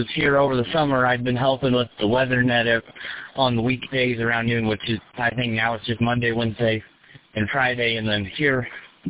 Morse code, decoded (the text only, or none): none